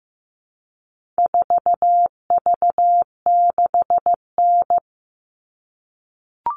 Morse code, heard as a keyed tone, about 15 wpm, 700 Hz. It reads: 4V6N
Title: All Cut Numbers - 4-Digit 15wpm